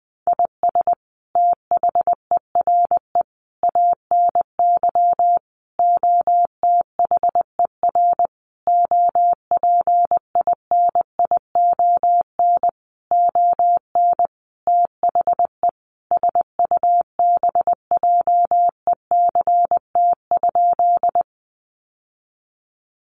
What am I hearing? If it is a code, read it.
Morse code, heard as a keyed tone, about 20 wpm, 700 Hz. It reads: IS THERE ANY OTHER OPINION ON THE SUBJECT?
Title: Sentences from Top 1200 Words 20wpm